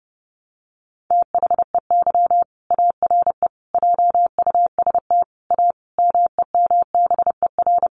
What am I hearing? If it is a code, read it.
Morse code, heard as a keyed tone, about 30 wpm, 700 Hz. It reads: THEY ARE JUST A MEMBER
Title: Sentences from Top 1100 Words 30wpm